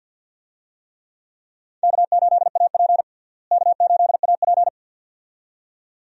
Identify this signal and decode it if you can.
Morse code, heard as a keyed tone, about 52 wpm, 700 Hz. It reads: K8AP K8AP
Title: USA Call Signs - 4 Characters - Repeated 2x - 50wpm